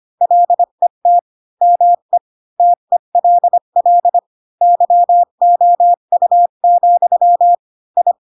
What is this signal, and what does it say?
Morse code, heard as a keyed tone, about 25 wpm, 700 Hz. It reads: LET ME TELL YOU, I
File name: Pinocchio / Pinocchio ch4 20 25 ik1hge